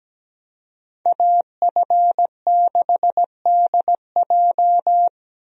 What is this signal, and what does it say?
Morse code, heard as a keyed tone, about 17 wpm, 700 Hz. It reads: AF6DJ
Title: USA Call Signs - 5 Characters - Repeated 3x - 17wpm